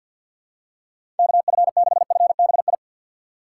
Morse code, heard as a keyed tone, about 47 wpm, 700 Hz. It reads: KF6FBI
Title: USA Call Signs - 6x Character Spacing - 50wpm